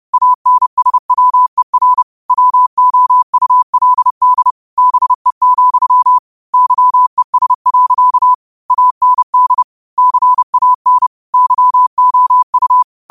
Morse code, heard as a keyed tone, about 30 wpm, 1000 Hz. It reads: ANSWER WOULD BE, YES. AND CAN YOU
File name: qotdfiles / 30wpm